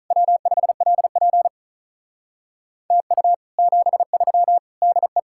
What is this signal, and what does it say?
Morse code, heard as a keyed tone, about 46 wpm, 700 Hz. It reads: W5LP TU 73 DE